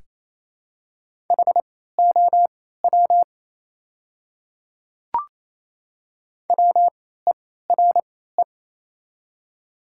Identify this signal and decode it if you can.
Morse code, heard as a keyed tone, about 28 wpm, spaced as for 16 wpm, 700 Hz. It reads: HOW WERE
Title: 3x Character Spacing - Top 100 Words - 28wpm